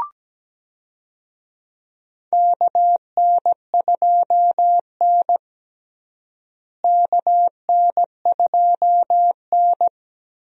Morse code, heard as a keyed tone, about 17 wpm, 700 Hz. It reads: KN2N KN2N
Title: USA Call Signs - 4 Characters - Repeated 3x - 17wpm